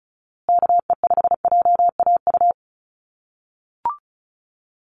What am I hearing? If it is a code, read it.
Morse code, heard as a keyed tone, about 34 wpm, 700 Hz. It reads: KE5JAU
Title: USA Call Signs Repeated 2x 35wpm